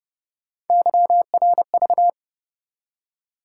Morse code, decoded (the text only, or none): YRV